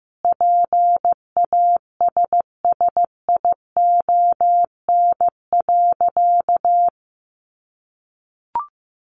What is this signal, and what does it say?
Morse code, heard as a keyed tone, about 14 wpm, 700 Hz. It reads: PASSION.
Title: Sentences from Top 1300 Words 15wpm